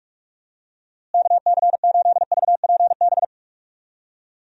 Morse code, heard as a keyed tone, about 46 wpm, 700 Hz. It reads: KC8VPB
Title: USA Call Signs - 6 Characters - Repeated 2x - 45wpm